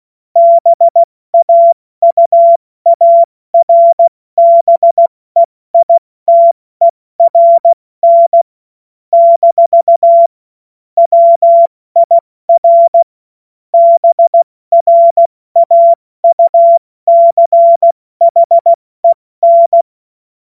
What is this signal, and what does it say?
Morse code, heard as a keyed tone, about 16 wpm, spaced as for 14 wpm, 680 Hz. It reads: BAUARBEITERN - WIR BRAUCHEN
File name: Morsekurs / Witze / Witz 01. August 2023 70 F80